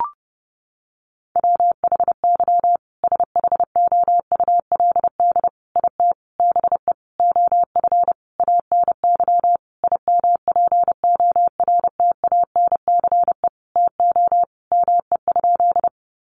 Morse code, heard as a keyed tone, about 30 wpm, 700 Hz. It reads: WHY SHOULD IT BE OF ANY IMPORTANCE TO ME?
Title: Sentences from Top 1200 Words 30wpm